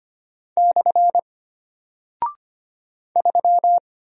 Morse code, heard as a keyed tone, about 25 wpm, 700 Hz. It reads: / 3
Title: Single Character (Letter:Number:Punctuation:Prosign) Alt 25wpm